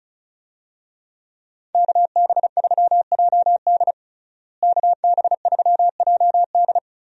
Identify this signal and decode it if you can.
Morse code, heard as a keyed tone, about 36 wpm, 700 Hz. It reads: KB3JD KB3JD
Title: USA Call Signs - 5 Characters - Repeated 3x - 35wpm